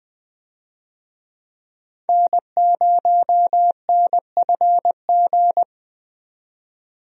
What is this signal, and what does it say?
Morse code, heard as a keyed tone, about 20 wpm, 700 Hz. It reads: N0NFG
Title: USA Call Signs - 5 Characters - 20wpm